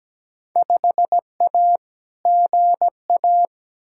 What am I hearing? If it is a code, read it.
Morse code, heard as a keyed tone, about 17 wpm, 700 Hz. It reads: 5A GA
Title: Field Day Exchanges Encore - 17wpm